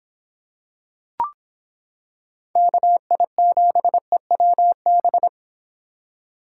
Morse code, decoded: KI7EWB

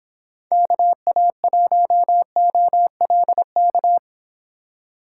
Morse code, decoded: KA1OLK

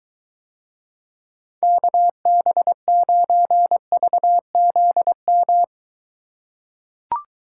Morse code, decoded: KB9VZM

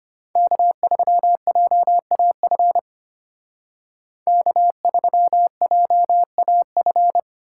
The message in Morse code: K3JAF K3JAF

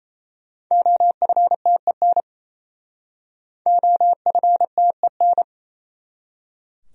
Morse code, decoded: OFTEN OFTEN

